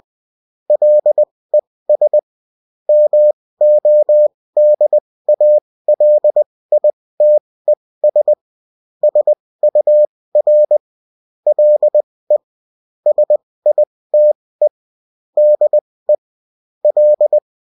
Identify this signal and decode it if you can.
Morse code, heard as a keyed tone, about 20 wpm, spaced as for 16 wpm, 600 Hz. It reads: LES MODALITES SUR LE SITE DE L